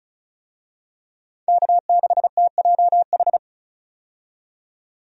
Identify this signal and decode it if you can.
Morse code, heard as a keyed tone, about 34 wpm, 700 Hz. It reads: K6TJH
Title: USA Call Signs - 5 Characters - 35wpm